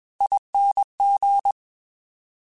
Morse code, heard as a keyed tone, about 21 wpm, 800 Hz. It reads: ING